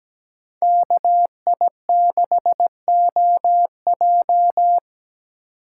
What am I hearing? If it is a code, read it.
Morse code, heard as a keyed tone, about 17 wpm, 700 Hz. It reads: KI6OJ
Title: USA Call Signs - 5 Characters - 2x Character Spacing - 17wpm